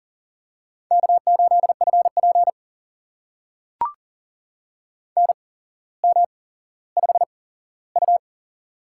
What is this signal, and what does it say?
Morse code, heard as a keyed tone, about 39 wpm, spaced as for 17 wpm, 700 Hz. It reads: K8FP NM5U
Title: USA Call Signs - 4 Characters - 8x Character Spacing - 40wpm